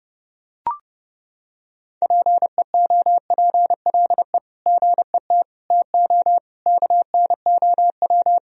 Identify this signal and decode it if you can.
Morse code, heard as a keyed tone, about 30 wpm, 700 Hz. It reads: PEOPLE GET TO KNOW
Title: Sentences from Top 1000 Words - Review Part 1 - 30wpm